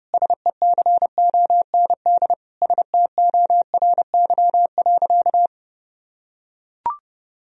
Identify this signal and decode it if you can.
Morse code, heard as a keyed tone, about 30 wpm, 700 Hz. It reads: SECOND STORY.
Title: Sentences from Top 1000 Words - Review Part 3 - 30wpm